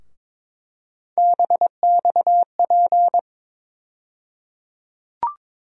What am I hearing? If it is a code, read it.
Morse code, heard as a keyed tone, about 22 wpm, 700 Hz. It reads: BXP
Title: USA Call Sign Postfixes - 3 Letters 22wpm